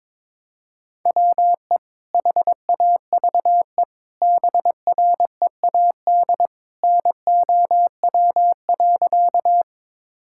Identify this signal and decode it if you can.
Morse code, heard as a keyed tone, about 22 wpm, 700 Hz. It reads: WE HAVE BREAD NOW.
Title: Sentences from Top 1600 Words 22wpm